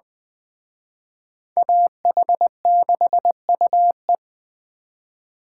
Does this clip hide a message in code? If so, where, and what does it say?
Yes: Morse code from 1.6 to 4.2 s, heard as a keyed tone, about 20 wpm, 700 Hz. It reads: AH6UE